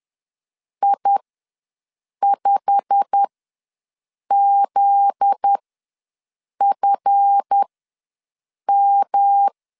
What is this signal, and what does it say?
Morse code, heard as a keyed tone, about 10 wpm, spaced as for 6 wpm, 800 Hz. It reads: I5ZFM
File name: LEZIONI / 9 28 ws 10wpm 4124